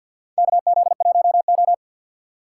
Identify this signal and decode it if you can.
Morse code, heard as a keyed tone, about 48 wpm, 700 Hz. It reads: KZ1O
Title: USA Call Signs - 4 Characters - 3x Character Spacing - 50wpm